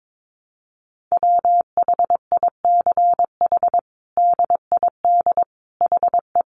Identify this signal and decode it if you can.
Morse code, heard as a keyed tone, about 22 wpm, 700 Hz. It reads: WHICH DID HE
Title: Sentences from Top 1100 Words 22wpm